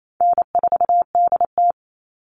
Morse code, heard as a keyed tone, about 28 wpm, 700 Hz. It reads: N4DT
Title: Call Signs Repeated 2x - 28wpm